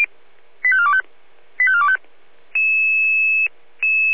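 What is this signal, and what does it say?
Five-tone selective call (ZVEI), heard as quick runs of five tones, tones of about 70 ms. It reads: 85425 85425
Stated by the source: ZVEI five-tone selective call 85410, received by SDR.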